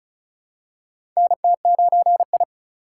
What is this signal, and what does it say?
Morse code, heard as a keyed tone, about 36 wpm, 700 Hz. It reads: NT9I
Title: USA Call Signs - 4 Characters - 2x Character Spacing - 35wpm